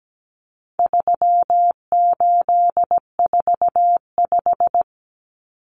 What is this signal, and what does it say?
Morse code, heard as a keyed tone, about 17 wpm, 700 Hz. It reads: 3845
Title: Common Cut Numbers - 4-Digit 17wpm